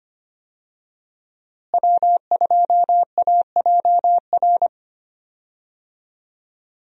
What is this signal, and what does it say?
Morse code, heard as a keyed tone, about 25 wpm, 700 Hz. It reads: W2AJR
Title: USA Call Signs - 5 Characters - Repeated 3x - 25wpm